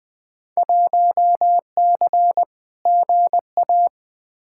Morse code, heard as a keyed tone, about 20 wpm, 700 Hz. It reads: 1C GA
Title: Field Day Exchanges Encore - 20wpm